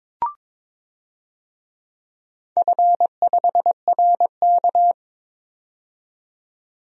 Morse code, heard as a keyed tone, about 22 wpm, 700 Hz. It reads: F5RK
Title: DXCC-100 Call Signs - 4 Characters - 22wpm